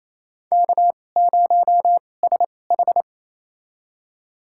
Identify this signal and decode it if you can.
Morse code, heard as a keyed tone, about 28 wpm, spaced as for 20 wpm, 700 Hz. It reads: K0SH
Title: USA Call Signs - 4 Characters - 2x Character Spacing - 28wpm